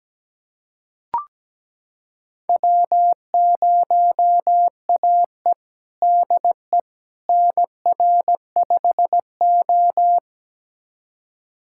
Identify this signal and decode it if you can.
Morse code, heard as a keyed tone, about 17 wpm, 700 Hz. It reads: W0AE DE NR5O
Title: USA Callsign DE Callsign - 4 Characters - 17wpm